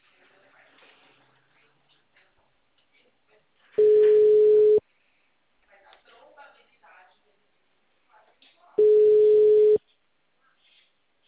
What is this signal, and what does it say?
Ringback tone (Europe, 425 Hz), with speech.
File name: out- -2005- - - .3131